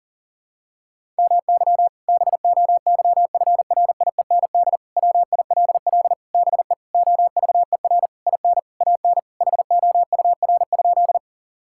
Morse code, heard as a keyed tone, about 40 wpm, 700 Hz. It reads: MY BOYFRIEND WILL BE OVER IN AN HOUR?
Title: Sentences from Top 1000 Words - Review Part 5 - 40wpm